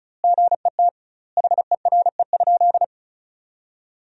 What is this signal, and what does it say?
Morse code, heard as a keyed tone, about 36 wpm, 700 Hz. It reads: GET HERE?
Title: 2x Word-Spacing - Sentences from Top 200 Words - 35wpm